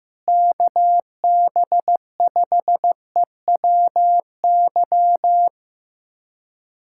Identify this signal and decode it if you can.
Morse code, heard as a keyed tone, about 16 wpm, 700 Hz. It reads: KB5EWY
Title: USA Callsign DE Callsign - 6 Characters - 15wpm